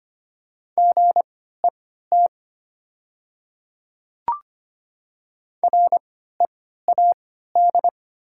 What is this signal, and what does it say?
Morse code, heard as a keyed tone, about 25 wpm, spaced as for 14 wpm, 700 Hz. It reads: GET READ
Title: 3x Character Spacing - Top 200 Words - 25wpm